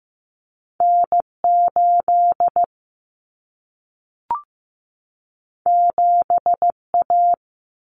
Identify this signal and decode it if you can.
Morse code, heard as a keyed tone, about 15 wpm, 700 Hz. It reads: N8 7A
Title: Common Cut Numbers - 2-Digit 15wpm